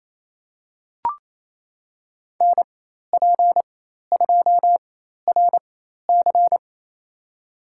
Morse code, heard as a keyed tone, about 28 wpm, spaced as for 13 wpm, 700 Hz. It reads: NP2RC
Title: USA Call Signs - 5 Characters - 4x Character Spacing - 28wpm